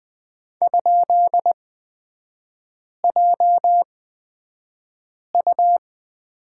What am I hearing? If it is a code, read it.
Morse code, heard as a keyed tone, about 20 wpm, spaced as for 5 wpm, 700 Hz. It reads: ?JU
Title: 243 Lesson 39e - Letter z - Three Characters 20wpm